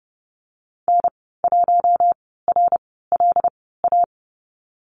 Morse code, heard as a keyed tone, about 30 wpm, spaced as for 17 wpm, 700 Hz. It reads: N1RLA
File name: Lesson 14g - Number 2 - Call Signs 30wpm